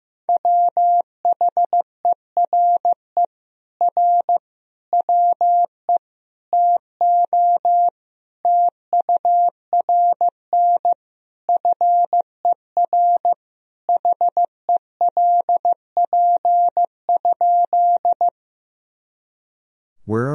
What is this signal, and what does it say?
Morse code, heard as a keyed tone, about 15 wpm, 700 Hz. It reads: WHERE R WE TO TURN FER HELP?
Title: Sentences from Top 200 Words and Abbreviations 15wpm